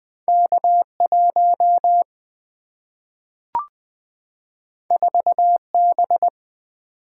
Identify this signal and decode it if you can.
Morse code, heard as a keyed tone, about 20 wpm, 700 Hz. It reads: K 1 4 B
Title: Single-Letter Single-Number 20wpm